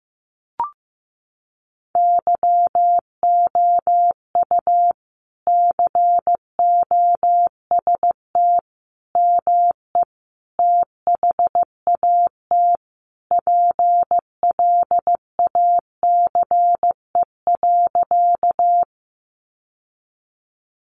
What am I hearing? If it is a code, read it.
Morse code, heard as a keyed tone, about 15 wpm, 700 Hz. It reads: YOU COST ME THAT PLACE.
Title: Sentences from Top 800 Words 15wpm